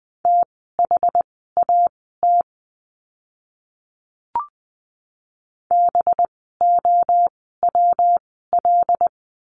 Morse code, heard as a keyed tone, about 20 wpm, spaced as for 14 wpm, 700 Hz. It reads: THAT BOWL
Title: 185 Lesson 30f - Letter b - Words 20wpm